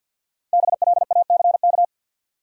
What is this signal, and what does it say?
Morse code, heard as a keyed tone, about 53 wpm, 700 Hz. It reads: BLACK
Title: Sets of 5 Words Encore - New Word Drill - 50wpm